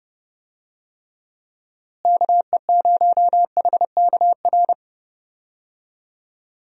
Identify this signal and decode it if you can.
Morse code, heard as a keyed tone, about 31 wpm, 700 Hz. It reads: KE0HKR
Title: USA Call Signs - 6 Characters - Repeated 3x - 30wpm